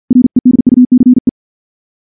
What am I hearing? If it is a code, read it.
Morse code, heard as a keyed tone, about 46 wpm, 250 Hz. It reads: REDUCE
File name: morse / morse-training-50WPM